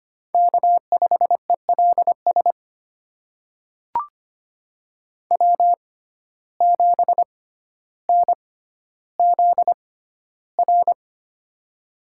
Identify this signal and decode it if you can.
Morse code, heard as a keyed tone, about 25 wpm, spaced as for 13 wpm, 690 Hz. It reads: K5ELS W7NZR
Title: USA Call Signs - 5 Characters - 6x Character Spacing - 25wpm